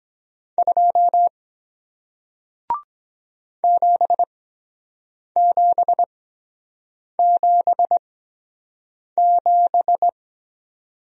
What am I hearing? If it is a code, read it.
Morse code, heard as a keyed tone, about 23 wpm, spaced as for 7 wpm, 700 Hz. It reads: 2 7777